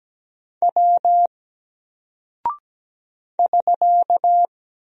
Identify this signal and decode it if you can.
Morse code, heard as a keyed tone, about 17 wpm, 700 Hz. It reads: W <SK>
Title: Single Character (Letter:Number:Punctuation:Prosign) 17wpm